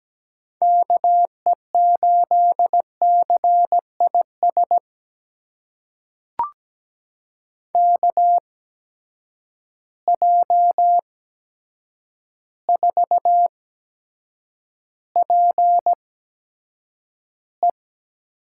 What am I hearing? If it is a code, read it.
Morse code, heard as a keyed tone, about 17 wpm, spaced as for 8 wpm, 700 Hz. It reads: KE8CIS KJ4PE
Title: USA Call Signs - 8x Character Spacing - 17wpm